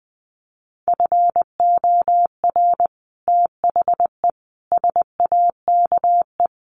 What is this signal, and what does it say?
Morse code, heard as a keyed tone, about 20 wpm, 700 Hz. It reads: FOR THE SAKE
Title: Sets of 5 Words Encore 20wpm